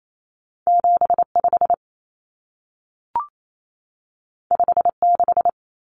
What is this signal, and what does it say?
Morse code, heard as a keyed tone, about 28 wpm, 700 Hz. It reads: 75 56
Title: Common Cut Numbers - 2-Digit 28wpm